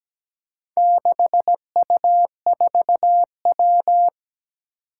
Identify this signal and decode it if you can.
Morse code, heard as a keyed tone, about 17 wpm, 700 Hz. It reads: 6U4W
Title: Four Challenging Letter-Number Combinations 17wpm